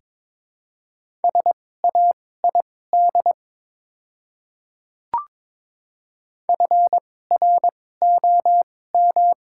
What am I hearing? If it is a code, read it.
Morse code, heard as a keyed tone, about 22 wpm, spaced as for 16 wpm, 700 Hz. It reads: SAID FROM
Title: 2x Character Spacing - Top 100 Words - 22wpm